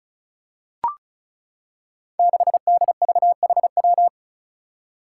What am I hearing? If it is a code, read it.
Morse code, heard as a keyed tone, about 36 wpm, 700 Hz. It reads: 6DVHW
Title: Five Challenging Letter-Number Combinations 35wpm